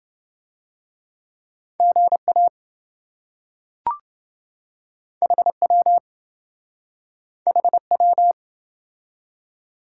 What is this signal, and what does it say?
Morse code, heard as a keyed tone, about 25 wpm, 700 Hz. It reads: GA HW HW
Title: Speed-Racing Top 180 QSO Elements 20wpm